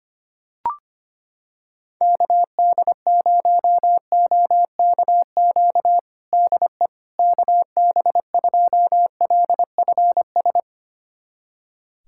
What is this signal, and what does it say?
Morse code, heard as a keyed tone, about 25 wpm, 700 Hz. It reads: KD0OKQ DE KB2LFS
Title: USA Callsign DE Callsign - 6 Characters - 25wpm